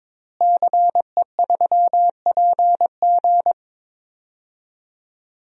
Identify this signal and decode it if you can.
Morse code, heard as a keyed tone, about 22 wpm, 700 Hz. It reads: CE3PG
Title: DXCC-100 Call Signs - 5 Characters - Repeated 2x - 22wpm